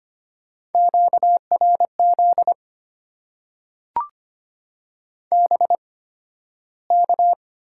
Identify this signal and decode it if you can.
Morse code, heard as a keyed tone, about 25 wpm, spaced as for 13 wpm, 700 Hz. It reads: QRZ BK
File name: Top 90 QSO Elements - 8x Character Spacing - 25wpm